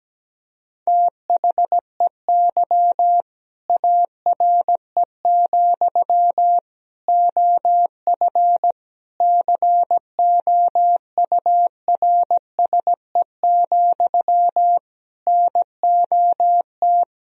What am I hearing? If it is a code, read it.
Morse code, heard as a keyed tone, about 17 wpm, 700 Hz. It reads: THEY ARE, OF COURSE, NOT
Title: Sentences from Top 1000 Words - Review Part 4 - 17wpm